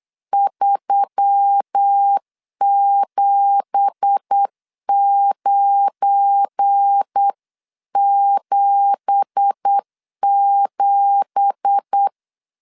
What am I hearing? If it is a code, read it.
Morse code, heard as a keyed tone, about 8 wpm, 800 Hz. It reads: 37977
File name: LEZIONI / 9 2867 8wpm 3440